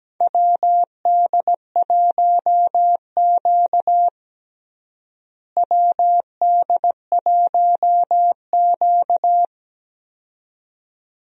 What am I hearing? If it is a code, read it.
Morse code, heard as a keyed tone, about 17 wpm, 700 Hz. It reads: WD1Q WD1Q
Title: USA Call Signs - 4 Characters - Repeated 2x - 17wpm